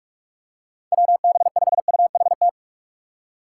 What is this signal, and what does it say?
Morse code, heard as a keyed tone, about 46 wpm, 700 Hz. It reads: WB5UHT
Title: USA Call Signs - 3x Character Spacing - 45wpm